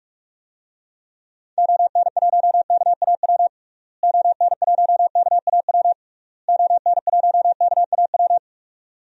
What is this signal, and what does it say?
Morse code, heard as a keyed tone, about 46 wpm, 700 Hz. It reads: ON1KAW ON1KAW ON1KAW
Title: DXCC-100 Call Signs - 6 Characters - Repeated 3x - 45wpm